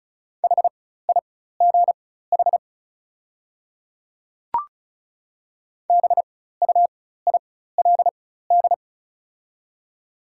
Morse code, heard as a keyed tone, about 34 wpm, spaced as for 16 wpm, 700 Hz. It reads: HIGH BUILD
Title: 4x Character Spacing - Top 300 Words - 35wpm